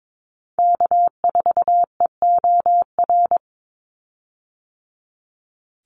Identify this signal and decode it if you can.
Morse code, heard as a keyed tone, about 22 wpm, 700 Hz. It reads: K4EOR
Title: Call Signs - 22wpm